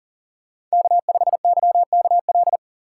Morse code, heard as a keyed tone, about 40 wpm, 700 Hz. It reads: K5YKL
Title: USA Call Signs - 5 Characters - 2x Character Spacing - 40wpm